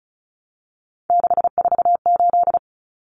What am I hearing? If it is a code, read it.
Morse code, heard as a keyed tone, about 34 wpm, 700 Hz. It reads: 648
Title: Numbers - 3-Digit 35wpm